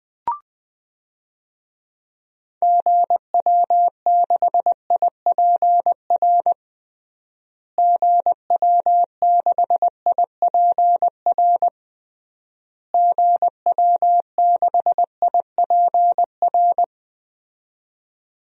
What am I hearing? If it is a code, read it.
Morse code, heard as a keyed tone, about 20 wpm, 700 Hz. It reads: GW6IPR GW6IPR GW6IPR
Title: DXCC-100 Call Signs - 6 Characters - Repeated 3x - 20wpm